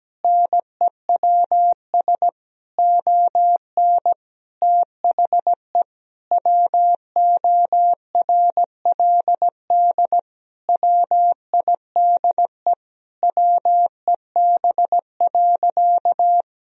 Morse code, heard as a keyed tone, about 17 wpm, 700 Hz. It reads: NEWS ON THE WORLD WIDE WEB.
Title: Sentences from Top 1000 Words - Review Part 4 - 17wpm